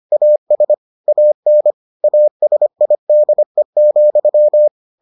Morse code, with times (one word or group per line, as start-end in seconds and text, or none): 0.1-0.7 AS
1.1-1.7 AN
2.0-4.7 ASIDE,